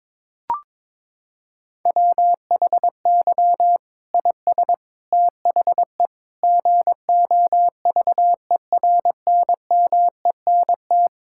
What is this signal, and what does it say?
Morse code, heard as a keyed tone, about 22 wpm, 700 Hz. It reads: WHY IS THE GOVERNMENT
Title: Sentences from Top 1000 Words - Review Part 5 - 22wpm